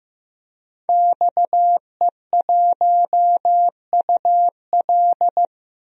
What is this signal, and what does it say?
Morse code, heard as a keyed tone, about 15 wpm, 700 Hz. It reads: XE1UL
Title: DXCC-100 Call Signs - 5 Characters - 15wpm